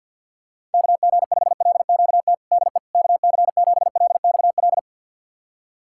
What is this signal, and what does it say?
Morse code, heard as a keyed tone, about 47 wpm, 700 Hz. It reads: KG5LQT DE KC7LXL
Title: USA Callsign DE Callsign - 6 Characters - 50wpm